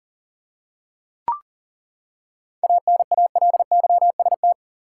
Morse code, heard as a keyed tone, about 40 wpm, 700 Hz. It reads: ANALYST
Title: Seven Letter Words - 40wpm